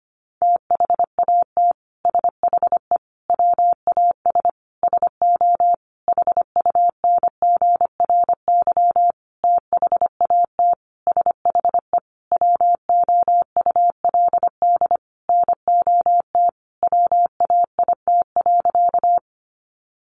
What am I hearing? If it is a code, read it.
Morse code, heard as a keyed tone, about 25 wpm, 700 Hz. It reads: THAT SHE WAS SO HUNGRY THAT SHE WOULD NOT WAIT.
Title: Sentences from Top 1400 Words 25wpm